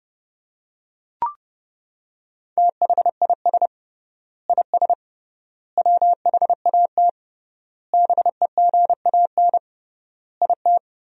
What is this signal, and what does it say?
Morse code, heard as a keyed tone, about 28 wpm, 700 Hz. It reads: THIS IS WHAT BEGAN IT
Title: 3x Word-Spacing - Sentences from Top 300 Words - 30wpm